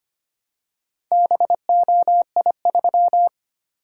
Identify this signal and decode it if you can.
Morse code, heard as a keyed tone, about 25 wpm, 700 Hz. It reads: BOI3
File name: practice / 25wpm / Four Letter-Number Combinations - 25wpm